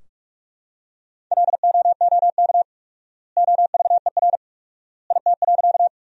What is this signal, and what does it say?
Morse code, heard as a keyed tone, about 46 wpm, 700 Hz. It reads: LOOK OVER IT.
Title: 4x Word-Spacing - Sentences from Top 100 Words - 45wpm